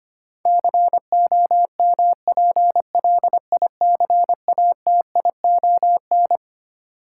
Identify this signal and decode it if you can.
Morse code, heard as a keyed tone, about 25 wpm, 700 Hz. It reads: COMPLICATION
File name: practice / 25wpm / Twelve Letter Words - 25wpm